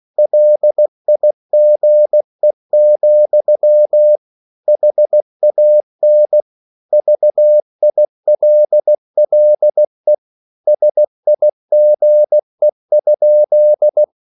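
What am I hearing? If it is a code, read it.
Morse code, heard as a keyed tone, about 16 wpm, 600 Hz. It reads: LIGE, HAN VILLE SIGE?
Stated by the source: Slow (QRS) Morse code at 16 wpm, from CW med Gnister.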